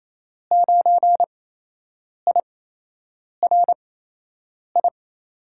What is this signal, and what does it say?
Morse code, heard as a keyed tone, about 28 wpm, spaced as for 8 wpm, 700 Hz. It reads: 9IRI
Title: USA Call Signs - 6 Characters - 8x Character Spacing - 28wpm